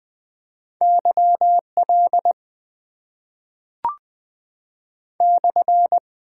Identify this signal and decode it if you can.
Morse code, heard as a keyed tone, about 20 wpm, 700 Hz. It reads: YL /
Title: Top 90 QSO Elements - 6x Character Spacing - 20wpm